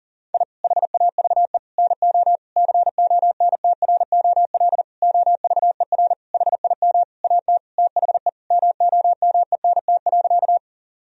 Morse code, heard as a keyed tone, about 40 wpm, 700 Hz. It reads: I HAVE NO CONTROL OVER HIM AT THE MOMENT.
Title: Sentences from Top 1000 Words - Review Part 3 - 40wpm